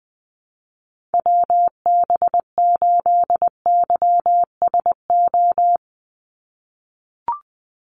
Morse code, heard as a keyed tone, about 20 wpm, 700 Hz. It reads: WB8YSO
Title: USA Call Signs 20wpm